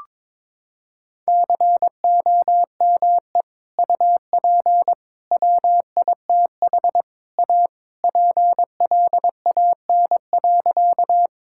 Morse code, heard as a keyed tone, about 22 wpm, 700 Hz. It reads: COME UP WITH A PLAN.
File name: practice / 22wpm / Sentences with 3-Word Phrases - 22wpm